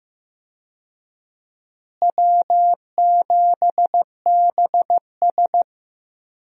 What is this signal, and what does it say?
Morse code, heard as a keyed tone, about 15 wpm, 700 Hz. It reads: W7BS
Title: USA Call Signs - 4 Characters - Repeated 2x - 15wpm